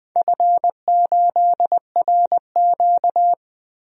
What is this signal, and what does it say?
Morse code, heard as a keyed tone, about 21 wpm, 700 Hz. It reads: F8RQ
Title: DXCC-100 Call Signs - 4 Characters - Repeated 3x - 20wpm